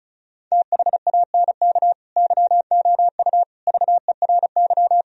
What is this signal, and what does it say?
Morse code, heard as a keyed tone, about 36 wpm, 700 Hz. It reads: THANK YOU VERY
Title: Sentences from Top 1000 Words - Review Part 4 - 35wpm